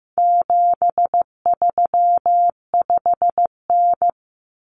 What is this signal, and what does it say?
Morse code, heard as a keyed tone, about 15 wpm, 700 Hz. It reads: 735N
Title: Common Cut Numbers - 4-Digit 15wpm